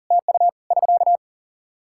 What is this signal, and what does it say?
Morse code, heard as a keyed tone, about 40 wpm, 700 Hz. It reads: TU <SK>
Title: Practice QSO - Exchange 4 - Ending - 40wpm